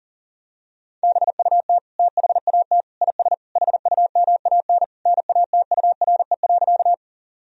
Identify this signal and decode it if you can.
Morse code, heard as a keyed tone, about 40 wpm, 700 Hz. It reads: BUT THAT IS HUMAN NATURE.
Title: Sentences from Top 1000 Words - Review Part 3 - 40wpm